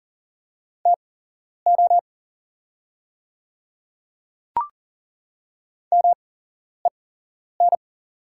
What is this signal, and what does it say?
Morse code, heard as a keyed tone, about 40 wpm, spaced as for 11 wpm, 700 Hz. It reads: TO MEN
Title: 8x Character Spacing - Top 200 Words - 40wpm